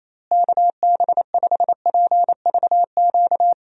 Morse code, heard as a keyed tone, about 28 wpm, 700 Hz. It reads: KB5PVQ
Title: USA Call Signs - 6 Characters - Repeated 3x - 28wpm